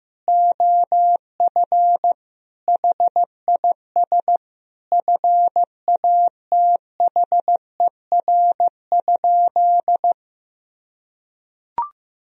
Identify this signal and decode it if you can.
Morse code, heard as a keyed tone, about 16 wpm, 700 Hz. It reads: OF HIS FATHER?
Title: Sentences from Top 1000 Words - Review Part 2 - 15wpm